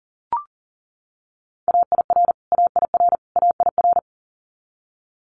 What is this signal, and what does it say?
Morse code, heard as a keyed tone, about 40 wpm, 700 Hz. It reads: AIR AIR AIR
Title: Top 200 Words Repeated 40wpm